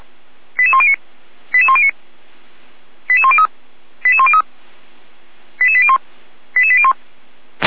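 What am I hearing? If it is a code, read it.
Five-tone selective call (ZVEI), heard as quick runs of five tones, tones of about 70 ms. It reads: 88109 88109 88103 88103 88801 88801
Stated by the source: ZVEI five-tone selective call received by SDR, fire brigade Orlamünde (feuerwehr-orlamuende).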